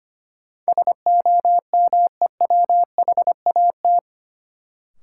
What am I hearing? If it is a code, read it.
Morse code, heard as a keyed tone, about 25 wpm, 700 Hz. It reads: SOMEWHAT
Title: Sentences from Top 1000 Words - Review Part 1 - New Word Drill - 25wpm